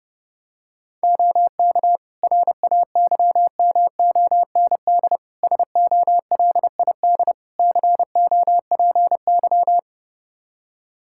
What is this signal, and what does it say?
Morse code, heard as a keyed tone, about 30 wpm, 700 Hz. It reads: OK RAYMOND SOLID COPY
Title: Practice QSO - Exchange 3 - More Info - 30wpm